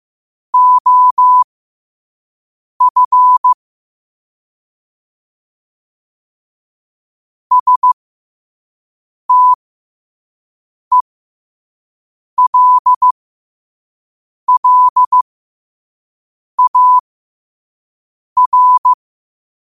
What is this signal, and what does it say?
Morse code, heard as a keyed tone, about 15 wpm, spaced as for 5 wpm, 1000 Hz. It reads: OF STELLAR
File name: qotdfiles / 5wpm